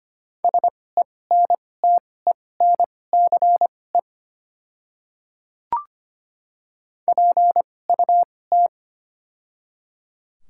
Morse code, heard as a keyed tone, about 25 wpm, spaced as for 18 wpm, 700 Hz. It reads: SENTENCE PUT
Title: 2x Character Spacing - Top 300 Words - 25wpm